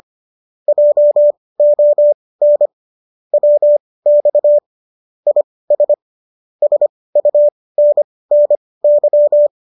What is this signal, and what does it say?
Morse code, heard as a keyed tone, about 25 wpm, spaced as for 18 wpm, 590 Hz. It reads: JON WX IS SUNNY
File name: qso207 18